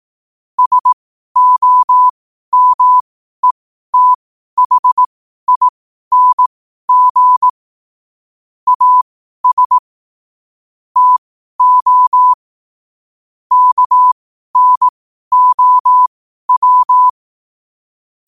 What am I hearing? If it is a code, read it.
Morse code, heard as a keyed tone, about 18 wpm, spaced as for 12 wpm, 1000 Hz. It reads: SOMETHING AS TO KNOW
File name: qotdfiles / 12wpm